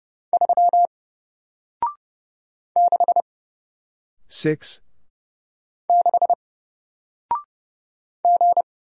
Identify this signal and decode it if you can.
Morse code, heard as a keyed tone, about 30 wpm, 700 Hz. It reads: 3 6 6 G
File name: Single-Letter-Number-30wpm